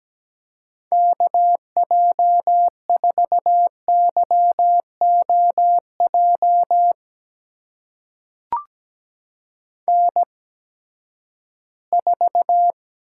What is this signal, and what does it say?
Morse code, heard as a keyed tone, about 17 wpm, spaced as for 12 wpm, 700 Hz. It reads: KJ4YOJ N4
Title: USA Call Signs - 8x Character Spacing - 17wpm